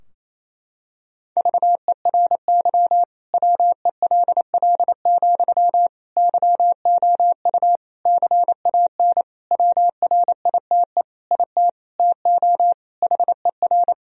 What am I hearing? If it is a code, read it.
Morse code, heard as a keyed tone, about 28 wpm, 700 Hz. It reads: VERY WELL, YOU CAN WRITE IT TO HER
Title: Sentences from Top 1000 Words - Review Part 4 - 28wpm